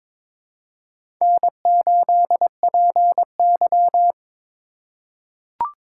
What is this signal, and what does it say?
Morse code, heard as a keyed tone, about 21 wpm, 700 Hz. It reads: N8PY
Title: USA Call Signs - 2x Character Spacing - 22wpm